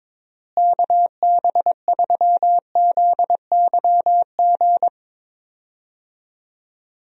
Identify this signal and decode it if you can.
Morse code, heard as a keyed tone, about 22 wpm, 700 Hz. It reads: KB3ZYG
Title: USA Call Signs - 6 Characters - Repeated 3x - 22wpm